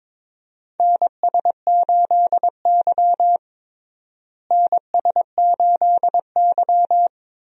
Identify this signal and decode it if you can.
Morse code, heard as a keyed tone, about 22 wpm, 700 Hz. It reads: NS8Y NS8Y
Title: USA Call Signs - 4 Characters - Repeated 2x - 22wpm